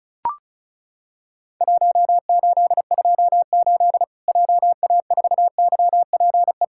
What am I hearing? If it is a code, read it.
Morse code, heard as a keyed tone, about 36 wpm, 700 Hz. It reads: 1828 JA4YPE
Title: Worked All Europe DX Contest - QTC Exchange - 35wpm